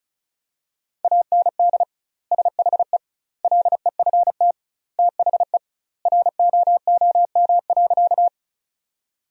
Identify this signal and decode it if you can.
Morse code, heard as a keyed tone, about 36 wpm, 700 Hz. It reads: AND SHE LEFT THE ROOM.
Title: 2x Word-Spacing - Sentences from Top 300 Words - 35wpm